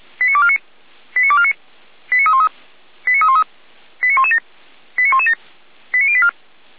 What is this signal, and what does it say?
Five-tone selective call (ZVEI), heard as quick runs of five tones, tones of about 70 ms. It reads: 89249 89249 89313 89313 89117 89117 89084
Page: /455/SDR/ZVEI/